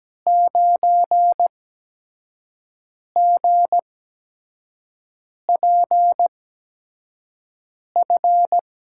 Morse code, heard as a keyed tone, about 17 wpm, spaced as for 5 wpm, 700 Hz. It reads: 9GPF
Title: USA Call Signs - 5 Characters - 8x Character Spacing - 17wpm